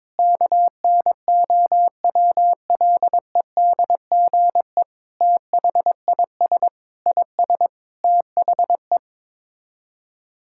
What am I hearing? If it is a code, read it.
Morse code, heard as a keyed tone, about 21 wpm, 700 Hz. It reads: KNOWLEDGE THIS IS THE
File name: Sets of 9 Words - 22wpm